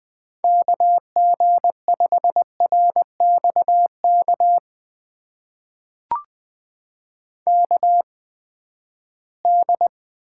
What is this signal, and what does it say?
Morse code, heard as a keyed tone, about 20 wpm, spaced as for 14 wpm, 700 Hz. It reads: KG5RXK KD
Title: USA Call Signs - 6 Characters - 8x Character Spacing - 20wpm